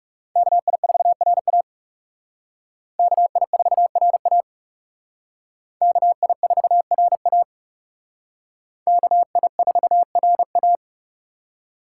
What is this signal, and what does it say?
Morse code, heard as a keyed tone, about 38 wpm, 700 Hz. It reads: KI4RA KI4RA KI4RA KI4RA